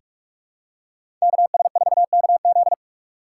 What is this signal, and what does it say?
Morse code, heard as a keyed tone, about 42 wpm, 700 Hz. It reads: KS4KZ
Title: USA Call Signs - 5 Characters - 3x Character Spacing - 45wpm